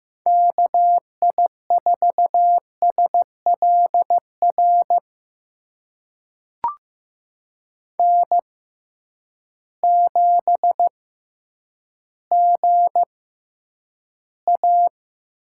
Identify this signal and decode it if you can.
Morse code, heard as a keyed tone, about 15 wpm, spaced as for 9 wpm, 700 Hz. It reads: KI4SLR N7GA
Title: USA Call Signs - 6x Character Spacing - 15wpm